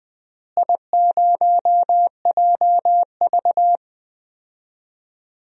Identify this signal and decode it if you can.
Morse code, heard as a keyed tone, about 20 wpm, 700 Hz. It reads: I0JV
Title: DXCC-100 Call Signs - 4 Characters - Repeated 2x - 20wpm